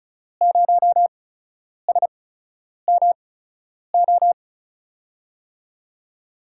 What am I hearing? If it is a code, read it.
Morse code, heard as a keyed tone, about 36 wpm, spaced as for 10 wpm, 700 Hz. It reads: 0SMO
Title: USA Call Signs - 6 Characters - 8x Character Spacing - 35wpm